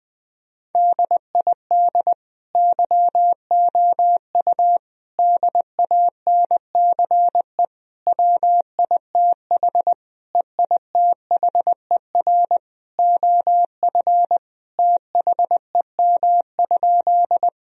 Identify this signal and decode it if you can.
Morse code, heard as a keyed tone, about 20 wpm, 700 Hz. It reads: DID YOU DANCE WITH EITHER OF THEM?
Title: Sentences from Top 1000 Words - Review Part 4 - 20wpm